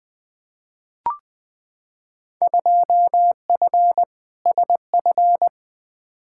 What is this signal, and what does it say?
Morse code, heard as a keyed tone, about 20 wpm, 700 Hz. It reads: 2F SF
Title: Field Day Exchanges - 20wpm